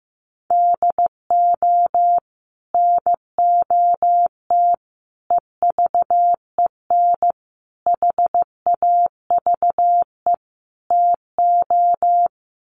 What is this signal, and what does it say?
Morse code, heard as a keyed tone, about 15 wpm, 700 Hz. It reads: DO NOT EVEN HAVE TO
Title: Sets of 5 Words Encore2 15wpm